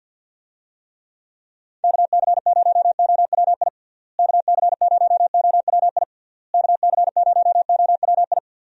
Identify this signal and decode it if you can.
Morse code, heard as a keyed tone, about 51 wpm, 700 Hz. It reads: KC0OWI KC0OWI KC0OWI
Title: USA Call Signs - 6 Characters - Repeated 3x - 50wpm